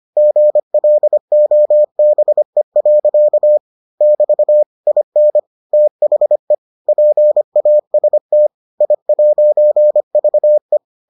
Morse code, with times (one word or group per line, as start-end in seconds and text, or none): 0.2-3.6 GLOBE.
4.0-4.6 <BT>
4.9-5.4 IN
5.7-6.5 THE
6.9-8.5 PAST
8.8-10.8 I'VE